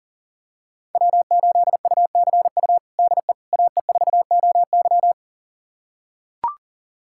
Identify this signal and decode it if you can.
Morse code, heard as a keyed tone, about 40 wpm, 700 Hz. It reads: W8UCU DE AE4OY
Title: USA Callsign DE Callsign 40wpm